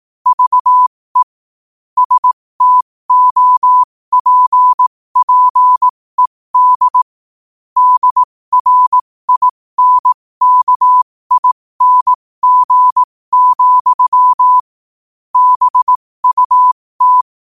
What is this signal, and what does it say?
Morse code, heard as a keyed tone, about 18 wpm, spaced as for 15 wpm, 1000 Hz. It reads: VE STOPPED DRINKING, BUT
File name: qotdfiles / 15wpm